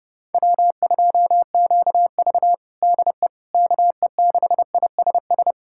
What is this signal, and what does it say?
Morse code, heard as a keyed tone, about 32 wpm, 700 Hz. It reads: W2QV DE KE6ISS
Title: USA Callsign DE Callsign 30wpm